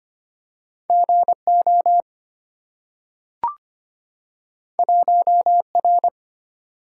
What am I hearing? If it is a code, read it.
Morse code, heard as a keyed tone, about 25 wpm, 700 Hz. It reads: GO 1R
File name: Two Letter-Number Combinations - 25wpm